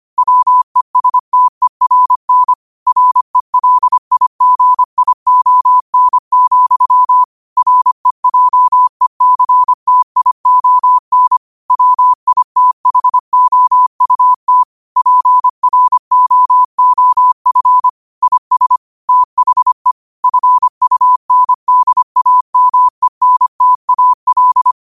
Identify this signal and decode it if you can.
Morse code, heard as a keyed tone, about 25 wpm, 1000 Hz. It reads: WESTERN RELIGION, REJECTION WITHOUT PROOF IS THE FUNDAMENTAL
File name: qotdfiles / 25wpm